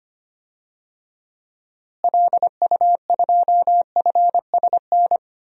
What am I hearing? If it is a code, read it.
Morse code, heard as a keyed tone, about 25 wpm, 700 Hz. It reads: LU2FSN
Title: DXCC-100 Call Signs - 6 Characters - Repeated 3x - 25wpm